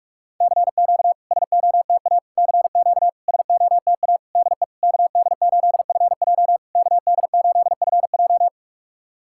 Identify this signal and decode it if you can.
Morse code, heard as a keyed tone, about 46 wpm, 700 Hz. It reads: CQ SOTA CQ SOTA DE KD8FJ KD8FJ
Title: SOTA QSO Practice 45wpm